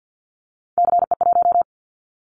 Morse code, heard as a keyed tone, about 50 wpm, 700 Hz. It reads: CE1
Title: Callsign Prefixes for Top 100 DXCC - 50wpm